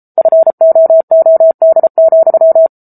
Morse code, heard as a keyed tone, about 34 wpm, 650 Hz. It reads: FOOD,
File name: Book Two / 07-Chapter 7 - 33 WPM